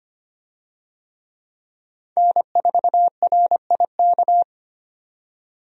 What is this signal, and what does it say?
Morse code, heard as a keyed tone, about 25 wpm, 700 Hz. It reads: N4RIK
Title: USA Call Signs - 5 Characters - 25wpm